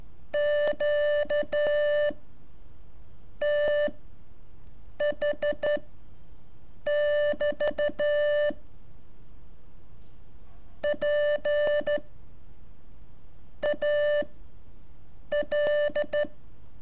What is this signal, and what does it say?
Morse code, heard as a keyed tone, about 13 wpm, spaced as for 5 wpm, 590 Hz. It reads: QTH <BT> PAL
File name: curso / ejercicio 18